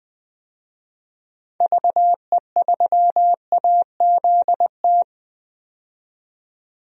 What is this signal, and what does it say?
Morse code, heard as a keyed tone, about 20 wpm, 700 Hz. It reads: VE3AZT
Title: DXCC-100 Call Signs - 6 Characters - 20wpm